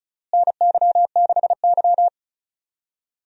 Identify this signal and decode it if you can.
Morse code, heard as a keyed tone, about 36 wpm, 700 Hz. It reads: NY6Y
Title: USA Call Signs - 4 Characters - Repeated 3x - 35wpm